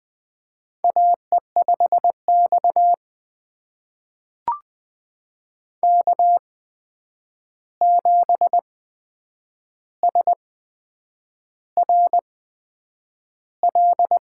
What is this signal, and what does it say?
Morse code, heard as a keyed tone, about 20 wpm, spaced as for 8 wpm, 700 Hz. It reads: AE5X K7SRL
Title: USA Call Signs - 8x Character Spacing - 20wpm